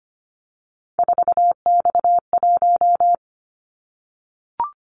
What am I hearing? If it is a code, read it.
Morse code, heard as a keyed tone, about 25 wpm, 690 Hz. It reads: 4X1
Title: Callsign Prefixes for Top 50 DXCC - 25wpm